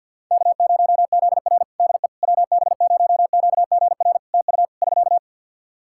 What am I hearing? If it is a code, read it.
Morse code, heard as a keyed tone, about 53 wpm, 700 Hz. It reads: K0ZR DE WB0QGR TU <SK>
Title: Practice QSO - Exchange 4 - Ending - 50wpm